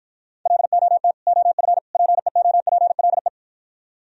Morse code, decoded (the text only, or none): LOT OF PEOPLE